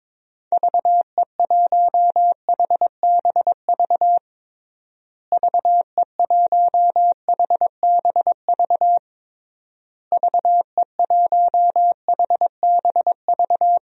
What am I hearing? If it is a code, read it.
Morse code, heard as a keyed tone, about 22 wpm, 700 Hz. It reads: VE1HBV VE1HBV VE1HBV
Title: DXCC-100 Call Signs - 6 Characters - Repeated 3x - 22wpm